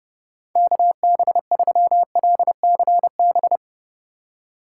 Morse code, heard as a keyed tone, about 30 wpm, 700 Hz. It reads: KB3LCB
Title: USA Call Signs - 6 Characters - 2x Character Spacing - 30wpm